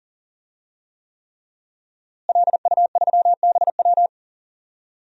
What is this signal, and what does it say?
Morse code, heard as a keyed tone, about 40 wpm, 700 Hz. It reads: LU3BW
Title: DXCC-100 Call Signs - 5 Characters - Repeated 3x - 40wpm